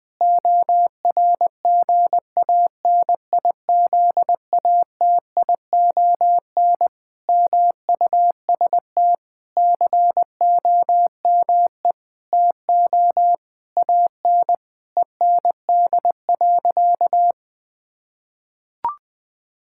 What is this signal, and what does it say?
Morse code, heard as a keyed tone, about 20 wpm, 700 Hz. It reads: ORGANIZATION MUST COME TO AN END.
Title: Sentences from Top 1000 Words - Review Part 5 - 20wpm